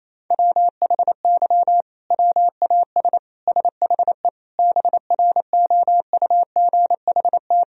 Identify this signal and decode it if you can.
Morse code, heard as a keyed tone, about 28 wpm, 700 Hz. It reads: WHY WAS SHE BROUGHT
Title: Sentences from Top 1000 Words - Review Part 3 - 28wpm